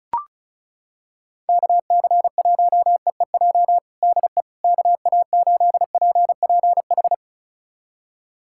Morse code, heard as a keyed tone, about 36 wpm, 700 Hz. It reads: KC1EEJ DE KA8PPH
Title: USA Callsign DE Callsign - 6 Characters - 35wpm